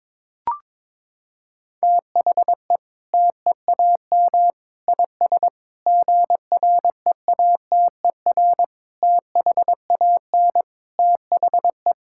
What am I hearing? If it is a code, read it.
Morse code, heard as a keyed tone, about 22 wpm, 700 Hz. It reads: THE TEAM IS GREATER THAN THE
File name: practice / 22wpm / Sentences with 8-Word Phrases - 22wpm